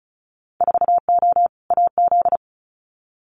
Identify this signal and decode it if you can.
Morse code, heard as a keyed tone, about 34 wpm, 700 Hz. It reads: 4O AZ
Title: Winter Field Day Exchanges Encore 35wpm